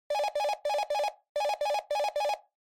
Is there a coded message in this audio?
No.